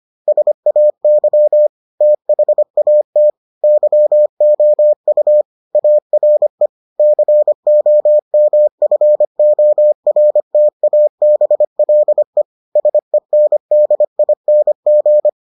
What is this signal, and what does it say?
Morse code, heard as a keyed tone, about 25 wpm, 600 Hz. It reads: SAY THAT YOU ARE COMFORTABLE SENDING